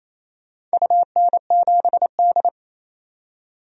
Morse code, heard as a keyed tone, about 28 wpm, 700 Hz. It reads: UN7D
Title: DXCC-100 Call Signs - 4 Characters - 28wpm